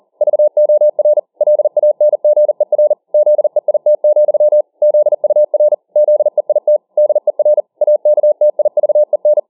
Morse code, heard as a keyed tone, about 40 wpm, 600 Hz. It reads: VOR LANGER ZEIT, ZUR ZEIT DER AKTIVEN